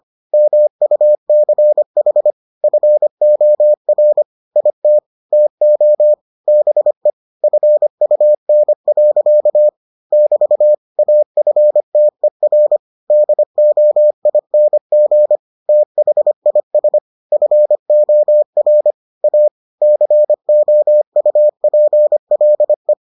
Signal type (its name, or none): Morse code